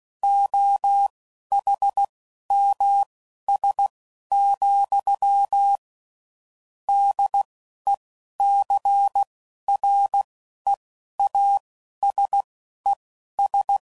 Morse code, heard as a keyed tone, about 16 wpm, spaced as for 12 wpm, 800 Hz. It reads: OHMS, DECREASES
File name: CW Code Groups @10WPM / Technical Subjects / XC